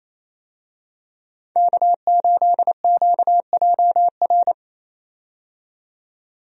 Morse code, heard as a keyed tone, about 28 wpm, 700 Hz. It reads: K8QJR
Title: USA Call Signs - 5 Characters - 28wpm